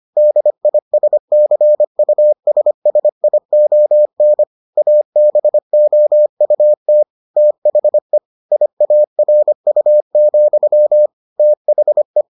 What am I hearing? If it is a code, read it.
Morse code, heard as a keyed tone, about 25 wpm, 600 Hz. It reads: DISCUSSION ABOUT THE IARU, THE